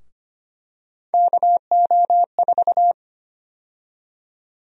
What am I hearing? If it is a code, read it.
Morse code, heard as a keyed tone, about 25 wpm, 700 Hz. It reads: KO4
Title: USA Call Sign Prefixes - Repeated 2x - 25wpm